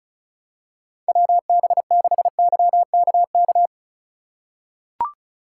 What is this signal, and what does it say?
Morse code, heard as a keyed tone, about 36 wpm, 700 Hz. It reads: WB6YKK
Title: USA Call Signs - 2x Character Spacing - 35wpm